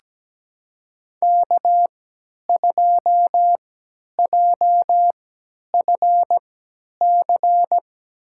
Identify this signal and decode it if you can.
Morse code, heard as a keyed tone, about 17 wpm, spaced as for 10 wpm, 700 Hz. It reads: K2JFC
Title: USA Call Signs - 5 Characters - 3x Character Spacing - 17wpm